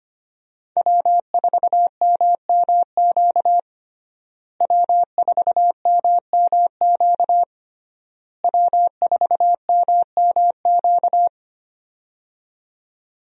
Morse code, heard as a keyed tone, about 25 wpm, 700 Hz. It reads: W4MMQ W4MMQ W4MMQ
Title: USA Call Signs - 5 Characters - Repeated 3x - 25wpm